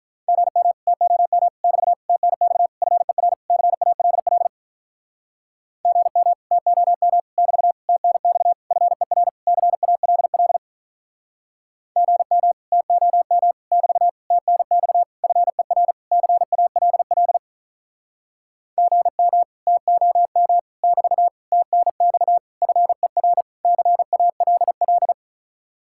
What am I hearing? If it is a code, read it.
Morse code, heard as a keyed tone, about 48 wpm, 700 Hz. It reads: GM TOM <BT> TNX FER CALL GM TOM <BT> TNX FER CALL GM TOM <BT> TNX FER CALL GM TOM <BT> TNX FER CALL